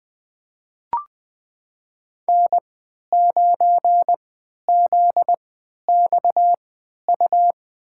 Morse code, heard as a keyed tone, about 20 wpm, spaced as for 11 wpm, 700 Hz. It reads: N9ZXU
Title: USA Call Signs - 3x Character Spacing - 20wpm